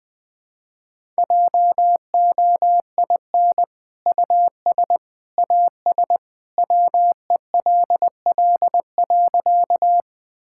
Morse code, heard as a keyed tone, about 20 wpm, 700 Hz. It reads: JOIN US AS WELL.
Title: Sentences with 2-Word Phrases - Spoken Beginning - 20wpm